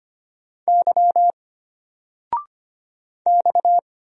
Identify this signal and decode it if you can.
Morse code, heard as a keyed tone, about 25 wpm, 700 Hz. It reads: Y X
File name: Single-Letter-Number-25wpm